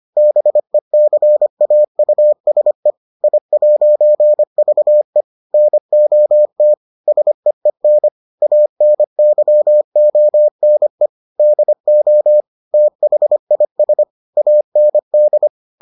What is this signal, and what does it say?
Morse code, heard as a keyed tone, about 25 wpm, 600 Hz. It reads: BECAUSE I'VE NOT SEEN ANYONE DO THIS AND